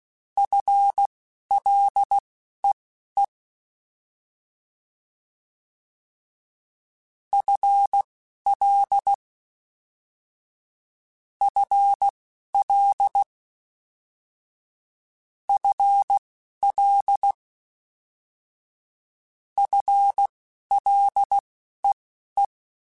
Morse code, heard as a keyed tone, about 16 wpm, spaced as for 12 wpm, 800 Hz. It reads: FLEE FL FL FL FLEE